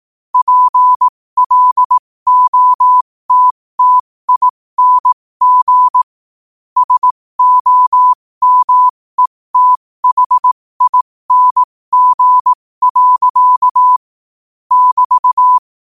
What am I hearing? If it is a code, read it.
Morse code, heard as a keyed tone, about 18 wpm, spaced as for 15 wpm, 1000 Hz. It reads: PLOTTING SOMETHING. <BT>
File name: qotdfiles / 15wpm